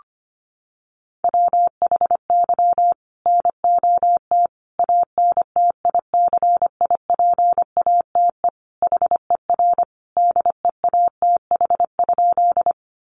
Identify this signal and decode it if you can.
Morse code, heard as a keyed tone, about 25 wpm, 700 Hz. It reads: WHY NOT ANTICIPATE HER DEATH?
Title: Sentences from Top 1500 Words 25wpm